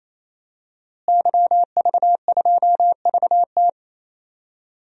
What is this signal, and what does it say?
Morse code, heard as a keyed tone, about 28 wpm, 700 Hz. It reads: YV2VT
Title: DXCC-100 Call Signs - 5 Characters - Repeated 3x - 28wpm